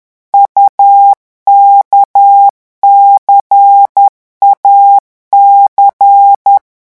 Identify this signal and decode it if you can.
Morse code, heard as a keyed tone, about 10 wpm, 800 Hz. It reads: UKCAC
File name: morse mp3 / revision2-vitesse 10 mots